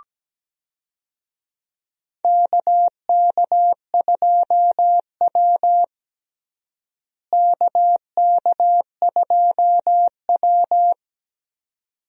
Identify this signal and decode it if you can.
Morse code, heard as a keyed tone, about 17 wpm, 700 Hz. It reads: KK2W KK2W
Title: USA Call Signs - 4 Characters - Repeated 2x - 17wpm